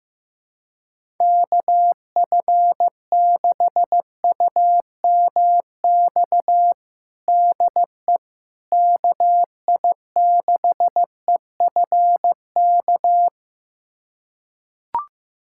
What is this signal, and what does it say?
Morse code, heard as a keyed tone, about 15 wpm, 700 Hz. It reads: KF6UMX DE KI6EFK
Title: USA Callsign DE Callsign - 6 Characters - 15wpm